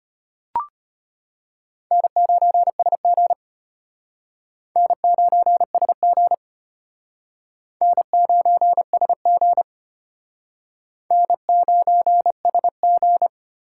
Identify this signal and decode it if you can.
Morse code, heard as a keyed tone, about 32 wpm, 700 Hz. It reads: N9SG N9SG N9SG N9SG